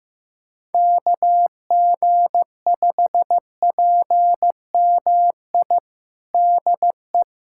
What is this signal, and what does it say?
Morse code, heard as a keyed tone, about 15 wpm, 700 Hz. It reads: KG5PMI DE
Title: USA Callsign DE Callsign - 6 Characters - 15wpm